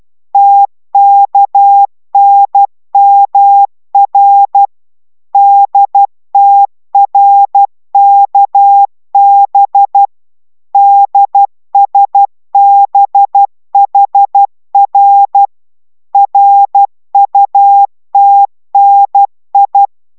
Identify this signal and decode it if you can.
Morse code, heard as a keyed tone, about 12 wpm, 800 Hz. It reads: TKNMR DTRKB DSBHR RUTNI